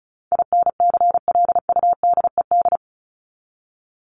Morse code, heard as a keyed tone, about 34 wpm, 700 Hz. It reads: INCLUDED
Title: ED Suffix - Top 100 Words with Base Words 35wpm